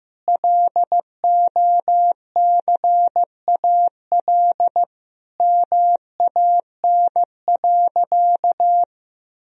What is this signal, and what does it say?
Morse code, heard as a keyed tone, about 15 wpm, 700 Hz. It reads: LOCAL MAN.
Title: Sentences from Top 1000 Words - Review Part 3 - 15wpm